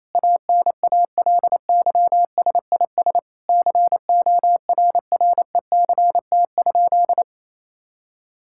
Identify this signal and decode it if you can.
Morse code, heard as a keyed tone, about 28 wpm, 700 Hz. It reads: ANALYSIS CORRECT?
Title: Sentences from Top 1000 Words - Review Part 5 - 28wpm